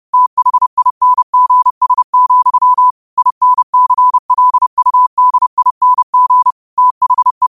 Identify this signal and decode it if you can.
Morse code, heard as a keyed tone, about 30 wpm, 1000 Hz. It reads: THINGS, INCLUDING THE
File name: qotdfiles / 30wpm